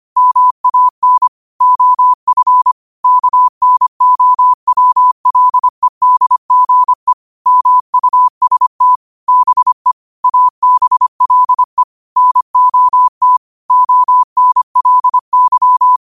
Morse code, heard as a keyed tone, about 25 wpm, 1000 Hz. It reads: MAN OF KNOWLEDGE MUST BE ABLE NOT ONLY
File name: qotdfiles / 25wpm